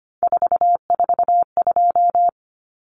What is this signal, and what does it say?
Morse code, heard as a keyed tone, about 25 wpm, 700 Hz. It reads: 442
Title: Numbers - 3-Digit 25wpm